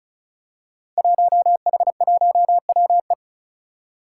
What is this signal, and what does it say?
Morse code, heard as a keyed tone, about 36 wpm, 700 Hz. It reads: 1H1WE
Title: Five Challenging Letter-Number Combinations 35wpm